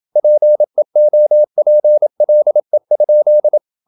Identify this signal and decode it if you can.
Morse code, heard as a keyed tone, about 27 wpm, 600 Hz. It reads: PEOPLE?
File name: Never 0027wpm